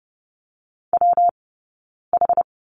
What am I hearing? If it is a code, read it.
Morse code, heard as a keyed tone, about 30 wpm, 700 Hz. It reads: W H
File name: Lesson 19d - Letter w - Two Characters 30wpm